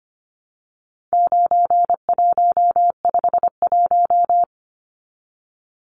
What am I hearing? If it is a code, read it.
Morse code, heard as a keyed tone, about 25 wpm, 700 Hz. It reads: 9151
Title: Numbers - 4-Digit 25wpm